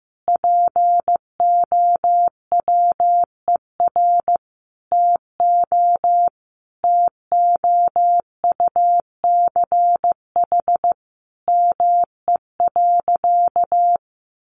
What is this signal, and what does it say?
Morse code, heard as a keyed tone, about 15 wpm, 700 Hz. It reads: POWER TO TOUCH ME.
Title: Sentences from Top 1200 Words 15wpm